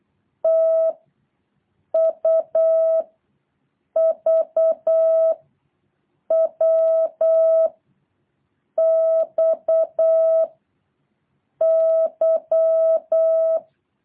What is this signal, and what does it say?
Morse code, heard as a keyed tone, about 8 wpm, 660 Hz. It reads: T U V W X Y
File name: jh8cbh / 2023cw